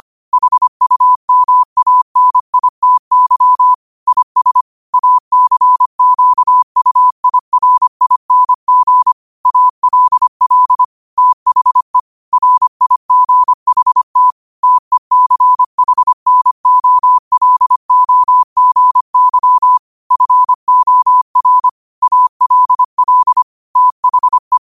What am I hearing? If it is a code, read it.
Morse code, heard as a keyed tone, about 25 wpm, 1000 Hz. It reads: HUMANITY IS ACQUIRING ALL THE RIGHT TECHNOLOGY FOR ALL THE